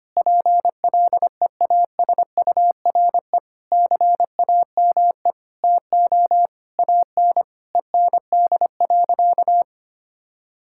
Morse code, heard as a keyed tone, about 25 wpm, 700 Hz. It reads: PLEASURE CAME TO AN END.
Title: Sentences from Top 1600 Words 25wpm